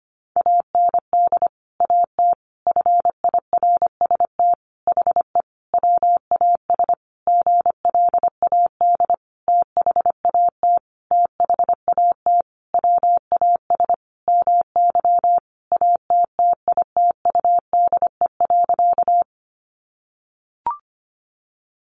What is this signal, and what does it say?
Morse code, heard as a keyed tone, about 25 wpm, 700 Hz. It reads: AND AT FIRST HE WAS GLAD THAT THAT WAS MY ATTITUDE.
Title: Sentences from Top 1300 Words 25wpm